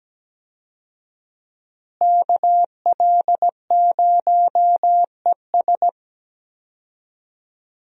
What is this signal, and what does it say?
Morse code, heard as a keyed tone, about 17 wpm, 700 Hz. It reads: KL0ES
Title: USA Call Signs - 5 Characters - Repeated 2x - 17wpm